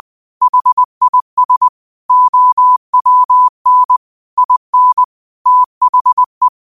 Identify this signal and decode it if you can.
Morse code, heard as a keyed tone, about 20 wpm, 1000 Hz. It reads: HIS OWN IN THE
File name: qotdfiles / 20wpm